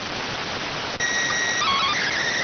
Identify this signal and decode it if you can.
SSTV (slow-scan television), PD 290.